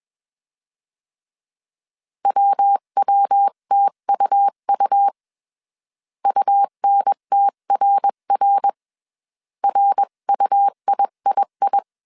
Morse code, heard as a keyed tone, about 21 wpm, 800 Hz. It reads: WWTUU UNTRR RUIII
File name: CW / DET / 1 3 1500C S B 20wpm 1754